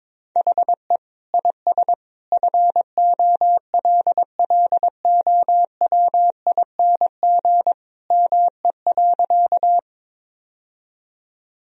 Morse code, heard as a keyed tone, about 22 wpm, 700 Hz. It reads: HE IS FOLLOWING ME.
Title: Sentences from Top 1000 Words - Review Part 4 - 22wpm